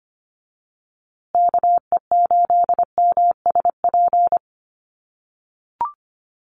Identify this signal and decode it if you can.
Morse code, heard as a keyed tone, about 25 wpm, 700 Hz. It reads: KE8MSP
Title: USA Call Signs Repeated 2x 25wpm